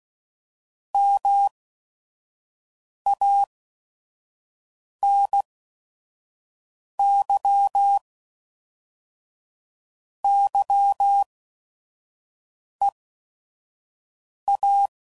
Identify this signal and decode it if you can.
Morse code, heard as a keyed tone, about 16 wpm, spaced as for 5 wpm, 800 Hz. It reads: MANY YEA
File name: CW Code Groups @5WPM / Technical Subjects / AND Gate